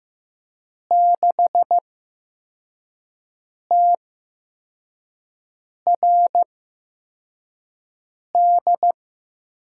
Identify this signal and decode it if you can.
Morse code, heard as a keyed tone, about 15 wpm, spaced as for 4 wpm, 700 Hz. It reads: 6TRD
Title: USA Call Signs - 8x Character Spacing - 15wpm